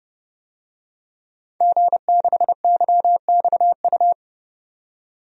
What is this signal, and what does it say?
Morse code, heard as a keyed tone, about 29 wpm, 700 Hz. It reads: G6YXU
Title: DXCC-100 Call Signs - 5 Characters - 30wpm